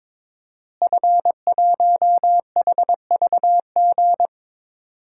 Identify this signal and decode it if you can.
Morse code, heard as a keyed tone, about 22 wpm, 700 Hz. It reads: F1HVG
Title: DXCC-100 Call Signs - 5 Characters - Repeated 2x - 22wpm